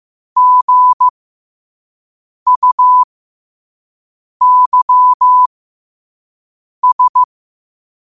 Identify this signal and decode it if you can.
Morse code, heard as a keyed tone, about 15 wpm, spaced as for 5 wpm, 1000 Hz. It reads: GUYS
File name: qotdfiles / 5wpm